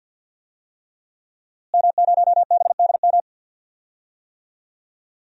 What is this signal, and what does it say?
Morse code, heard as a keyed tone, about 52 wpm, 700 Hz. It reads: M0BDM
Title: DXCC-100 Call Signs - 5 Characters - Repeated 3x - 50wpm